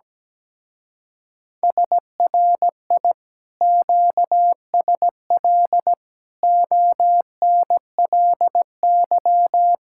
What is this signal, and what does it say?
Morse code, heard as a keyed tone, about 17 wpm, 700 Hz. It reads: SRI QSL ONLY
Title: Speed-Racing Common QSO Phrases - 17wpm